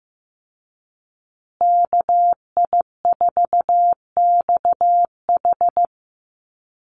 Morse code, heard as a keyed tone, about 15 wpm, 700 Hz. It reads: KI4XH
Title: Call Signs Repeated 3x - 15wpm